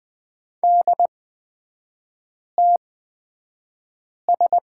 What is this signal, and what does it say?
Morse code, heard as a keyed tone, about 20 wpm, spaced as for 5 wpm, 690 Hz. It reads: DTS
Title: 229 Lesson 37e - Letter x - Three Characters 20wpm